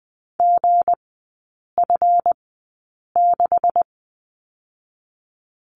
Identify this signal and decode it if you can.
Morse code, heard as a keyed tone, about 20 wpm, spaced as for 8 wpm, 700 Hz. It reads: GF6
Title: 159 Lesson 26e - Letter g - Three Characters 20wpm